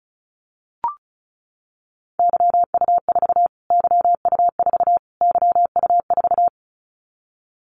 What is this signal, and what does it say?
Morse code, heard as a keyed tone, about 34 wpm, 700 Hz. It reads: YU4 YU4 YU4
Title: Callsign Prefixes for Top 25 DXCC - Repeated 3x - 35wpm